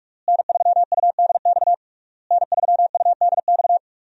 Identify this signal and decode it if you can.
Morse code, heard as a keyed tone, about 46 wpm, 700 Hz. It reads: N3UDX N3UDX
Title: USA Call Signs - 5 Characters - Repeated 3x - 45wpm